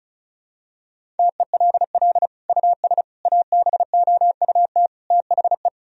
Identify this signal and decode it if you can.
Morse code, heard as a keyed tone, about 36 wpm, 700 Hz. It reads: TELL US ABOUT THE
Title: Sentences from Top 1000 Words - Review Part 4 - 35wpm